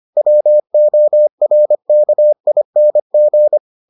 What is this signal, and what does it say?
Morse code, heard as a keyed tone, about 25 wpm, 600 Hz. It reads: WORKING